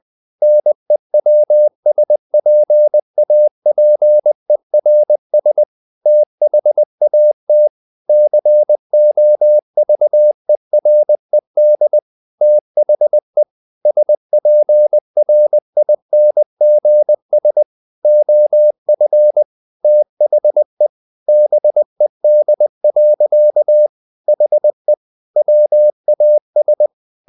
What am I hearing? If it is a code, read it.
Morse code, heard as a keyed tone, about 20 wpm, 600 Hz. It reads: NEWSPAPERS THAT COVERED THE SPRINGS OF THE BED. HE WAS
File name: the old man and the sea / the old man and the sea part 1 20wpm